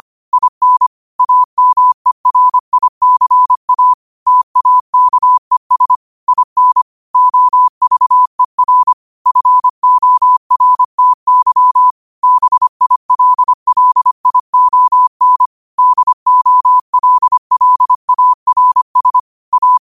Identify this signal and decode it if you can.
Morse code, heard as a keyed tone, about 25 wpm, 1000 Hz. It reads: IN AMERICA TAKES IN OVER FORTY BILLION DOLLARS A
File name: qotdfiles / 25wpm